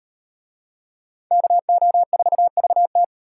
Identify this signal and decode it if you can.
Morse code, heard as a keyed tone, about 38 wpm, 700 Hz. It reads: KO4VT